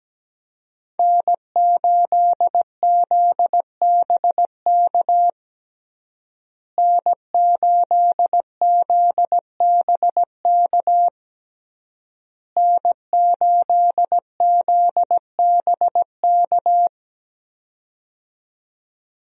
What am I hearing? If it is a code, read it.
Morse code, heard as a keyed tone, about 17 wpm, 700 Hz. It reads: N8ZBK N8ZBK N8ZBK
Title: USA Call Signs - 5 Characters - Repeated 3x - 17wpm